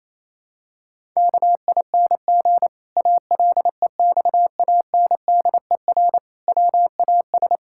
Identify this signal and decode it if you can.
Morse code, heard as a keyed tone, about 28 wpm, 700 Hz. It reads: KING ALEXANDER WAS
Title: Pangram Challenge v2 - 28wpm